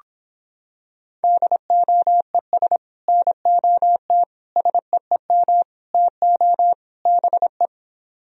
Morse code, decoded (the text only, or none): DOES NOT SEEM TO BE